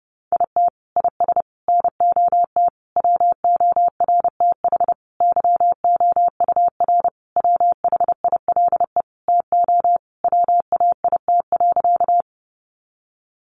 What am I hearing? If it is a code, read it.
Morse code, heard as a keyed tone, about 30 wpm, 700 Hz. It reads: IT IS NOT WORTH YOUR WHILE TO WAIT.
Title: Sentences from Top 800 Words 30wpm